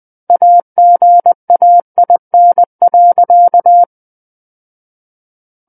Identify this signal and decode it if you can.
Morse code, heard as a keyed tone, about 20 wpm, 700 Hz. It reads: AGAIN.
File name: 20 WPM